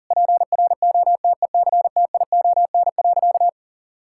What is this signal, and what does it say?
Morse code, heard as a keyed tone, about 37 wpm, 700 Hz. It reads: PROTECTION.
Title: Sentences from Top 1600 Words 40wpm